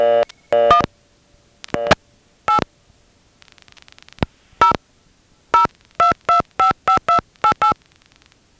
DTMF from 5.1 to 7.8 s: 02255288